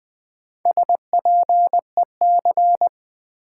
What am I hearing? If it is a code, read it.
Morse code, heard as a keyed tone, about 20 wpm, 700 Hz. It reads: SPEC
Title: Build A Word - 4 Letters At A Time - 11-Letter Words - 20wpm